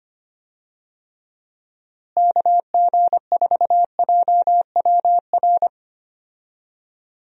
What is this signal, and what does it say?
Morse code, heard as a keyed tone, about 25 wpm, 700 Hz. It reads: KG4JWR
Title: USA Call Signs - 6 Characters - Repeated 2x - 25wpm